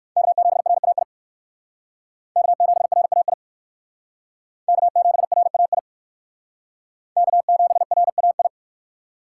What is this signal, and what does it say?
Morse code, heard as a keyed tone, about 57 wpm, 700 Hz. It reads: K7RAI K7RAI K7RAI K7RAI